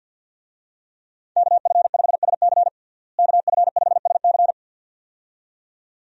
Morse code, heard as a keyed tone, about 49 wpm, 700 Hz. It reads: KF5SC KF5SC
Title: USA Call Signs - 5 Characters - Repeated 2x - 50wpm